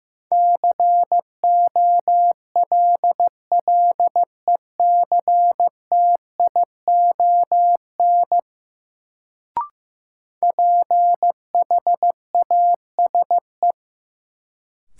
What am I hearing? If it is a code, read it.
Morse code, heard as a keyed tone, about 15 wpm, 700 Hz. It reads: COLLECTION PHASE
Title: Sentences from Top 1200 Words - New Word Drill Alt - 15wpm